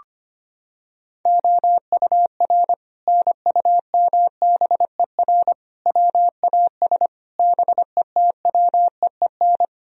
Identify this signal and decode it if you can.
Morse code, heard as a keyed tone, about 25 wpm, 700 Hz. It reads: OUR NUMBER WAS BETWEEN
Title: Sentences from Top 1000 Words - Review Part 2 - 25wpm